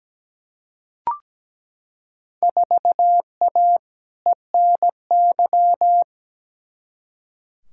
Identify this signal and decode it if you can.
Morse code, heard as a keyed tone, about 17 wpm, 700 Hz. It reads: 4A ENY
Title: Field Day Exchanges - 17wpm